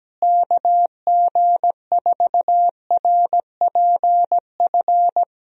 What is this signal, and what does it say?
Morse code, heard as a keyed tone, about 17 wpm, 700 Hz. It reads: KG4RPF
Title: USA Call Signs - 6 Characters - Repeated 3x - 17wpm